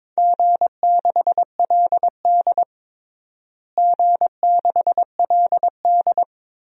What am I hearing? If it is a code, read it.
Morse code, heard as a keyed tone, about 22 wpm, 700 Hz. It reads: G6LD G6LD
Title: DXCC-100 Call Signs - 4 Characters - Repeated 2x - 22wpm